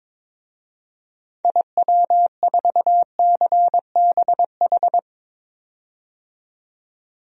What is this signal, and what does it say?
Morse code, heard as a keyed tone, about 22 wpm, 700 Hz. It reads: IW4CBH
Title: DXCC-100 Call Signs - 6 Characters - 22wpm